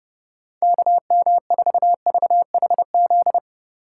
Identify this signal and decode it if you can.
Morse code, heard as a keyed tone, about 30 wpm, 700 Hz. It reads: KM4VHZ